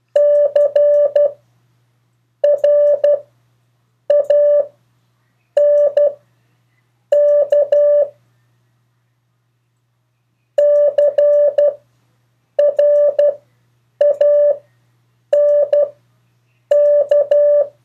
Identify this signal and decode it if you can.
Morse code, heard as a keyed tone, about 12 wpm, spaced as for 7 wpm, 580 Hz. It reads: CRANK CRANK
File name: 2of4-words-1